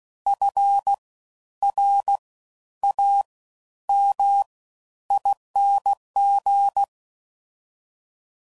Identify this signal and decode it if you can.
Morse code, heard as a keyed tone, about 16 wpm, spaced as for 10 wpm, 800 Hz. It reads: FRAMING
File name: Suffixes / Special INGs / Fast Ings